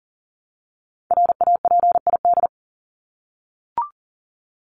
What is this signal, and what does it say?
Morse code, heard as a keyed tone, about 40 wpm, 700 Hz. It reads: RAPID
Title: LY Suffix - Top 100 Words with Base Words 40wpm